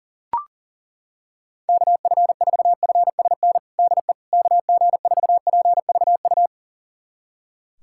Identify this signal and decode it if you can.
Morse code, heard as a keyed tone, about 40 wpm, 700 Hz. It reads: KF4FSN DE KG4PVU
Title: USA Callsign DE Callsign - 6 Characters - 40wpm